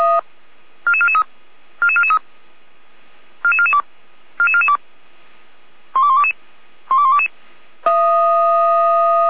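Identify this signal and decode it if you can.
Five-tone selective call (ZVEI), heard as quick runs of five tones, tones of about 70 ms. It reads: 44443 44443 44442 44442 12122 12122
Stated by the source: ZVEI five-tone selective call 12121 received by SDR, recorded on the feuerwehr-orlamuende fire brigade server.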